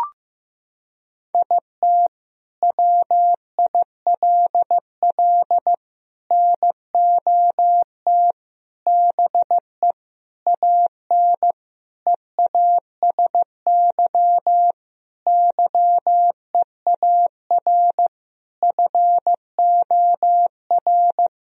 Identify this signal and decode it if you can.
Morse code, heard as a keyed tone, about 15 wpm, 700 Hz. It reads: IT WILL NOT BE AN EASY YEAR FOR
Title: Sentences from Top 1000 Words - Review Part 2 - 15wpm